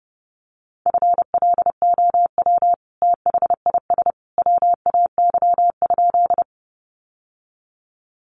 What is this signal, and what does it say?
Morse code, heard as a keyed tone, about 30 wpm, 700 Hz. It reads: FLOW THIS WAY?
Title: Sentences from Top 1200 Words 30wpm